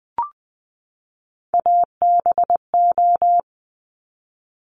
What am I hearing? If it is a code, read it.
Morse code, heard as a keyed tone, about 20 wpm, 700 Hz. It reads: ABO
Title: Top Three Letter Combinations in Words 20wpm